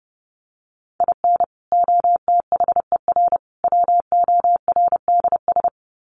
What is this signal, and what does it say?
Morse code, heard as a keyed tone, about 30 wpm, 700 Hz. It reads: IN OTHER WORDS
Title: Sets of 3 Words 30wpm